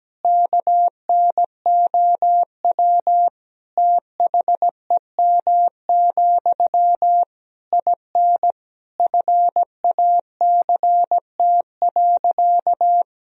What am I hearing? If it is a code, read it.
Morse code, heard as a keyed tone, about 17 wpm, 700 Hz. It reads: KNOW THEM, IN FACT.
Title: Sentences from Top 1000 Words - Review Part 3 - 17wpm